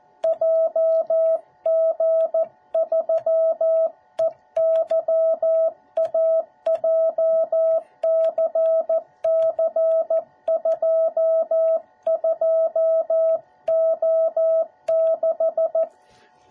Morse code, heard as a keyed tone, about 14 wpm, 650 Hz. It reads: JG3EYA JCC22O6